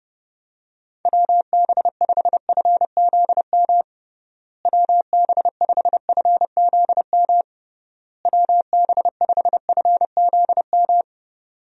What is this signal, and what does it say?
Morse code, heard as a keyed tone, about 30 wpm, 700 Hz. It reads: WB5FZM WB5FZM WB5FZM
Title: USA Call Signs - 6 Characters - Repeated 3x - 30wpm